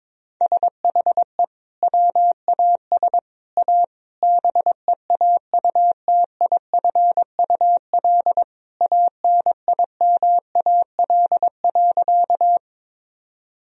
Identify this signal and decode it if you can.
Morse code, heard as a keyed tone, about 21 wpm, 700 Hz. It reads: SHE WAS A BEAUTIFUL ANIMAL.
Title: Sentences from Top 1000 Words - Review Part 1 - 22wpm